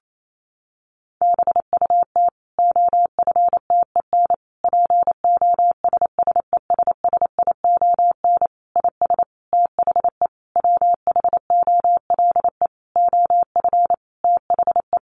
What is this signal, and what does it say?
Morse code, heard as a keyed tone, about 28 wpm, 700 Hz. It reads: BUT OFTEN POSSESSION IS THE WHOLE OF THE
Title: Sentences from Top 1400 Words 28wpm